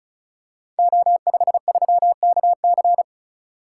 Morse code, heard as a keyed tone, about 36 wpm, 690 Hz. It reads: O53KC
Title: Code Groups - 5 Letter:Numbers - 35wpm